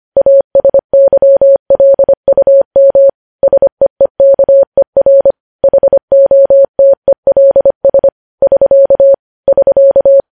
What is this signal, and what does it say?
Morse code, heard as a keyed tone, about 25 wpm, 550 Hz. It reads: ASYLUM SEEKER HOTELS <SK> <SK>
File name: podcast / News headlines-1x-25x25